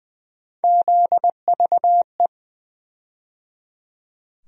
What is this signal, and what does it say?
Morse code, heard as a keyed tone, about 20 wpm, 700 Hz. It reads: ZVE
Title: USA Call Sign Postfixes - 3 Letters 20wpm